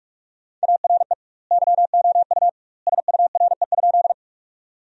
Morse code, heard as a keyed tone, about 42 wpm, 700 Hz. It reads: ARE YOU SURE?
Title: 2x Word-Spacing - Sentences from Top 300 Words - 45wpm